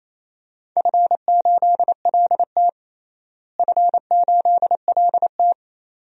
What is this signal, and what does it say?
Morse code, heard as a keyed tone, about 28 wpm, 700 Hz. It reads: F8LT F8LT
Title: DXCC-100 Call Signs - 4 Characters - Repeated 2x - 28wpm